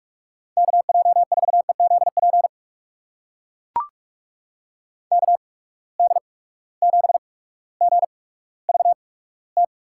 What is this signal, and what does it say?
Morse code, heard as a keyed tone, about 44 wpm, spaced as for 19 wpm, 700 Hz. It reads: KJ4EZP KD7GVT
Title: USA Call Signs - 6 Characters - 8x Character Spacing - 45wpm